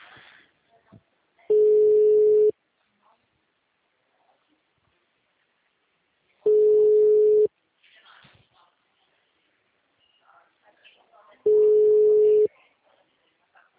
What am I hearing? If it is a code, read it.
Ringback tone (Europe, 425 Hz), with speech.
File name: out- -2012- - - .66380